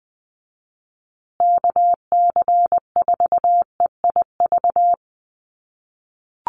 Morse code, heard as a keyed tone, about 20 wpm, 700 Hz. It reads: KC4EIV
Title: USA Call Signs Repeated 3x 20wpm